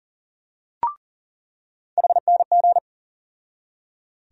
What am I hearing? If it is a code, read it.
Morse code, heard as a keyed tone, about 40 wpm, 700 Hz. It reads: HNG
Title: USA Call Sign Postfixes - 3 Letters 40wpm